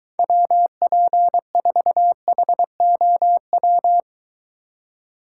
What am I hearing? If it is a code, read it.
Morse code, heard as a keyed tone, about 23 wpm, 700 Hz. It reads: WP4HOW